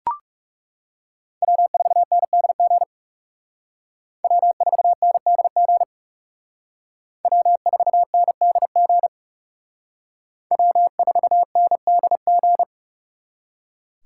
Morse code, heard as a keyed tone, about 36 wpm, 700 Hz. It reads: W4NDG W4NDG W4NDG W4NDG
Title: Speed Racing - USA 1x3 Call Signs 30wpm